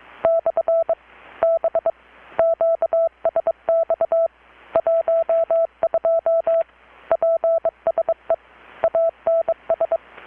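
Morse code, heard as a keyed tone, about 22 wpm, 660 Hz. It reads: / B QSX 12 PSE ANS
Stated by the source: Morse from KPH's Night of Nights XX maritime radiotelegraphy event, received in Chiapas, Mexico about 3,800 km away on a Kenwood R-600 with a grounded 10 m long-wire antenna.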